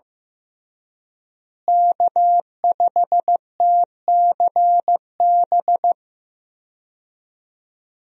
Morse code, heard as a keyed tone, about 15 wpm, 700 Hz. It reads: K5TCB
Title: USA Call Signs - 5 Characters - Repeated 2x - 15wpm